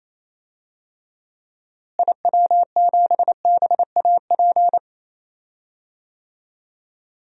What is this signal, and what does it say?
Morse code, heard as a keyed tone, about 28 wpm, 700 Hz. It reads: IW7BAP
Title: DXCC-100 Call Signs - 6 Characters - Repeated 3x - 28wpm